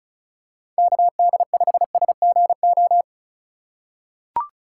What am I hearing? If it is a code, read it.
Morse code, heard as a keyed tone, about 36 wpm, 700 Hz. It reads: KD5SGO
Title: USA Call Signs - 6 Characters - 3x Character Spacing - 35wpm